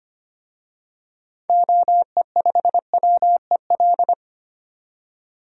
Morse code, heard as a keyed tone, about 25 wpm, 700 Hz. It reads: OE5WEL
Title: DXCC-100 Call Signs - 6 Characters - 25wpm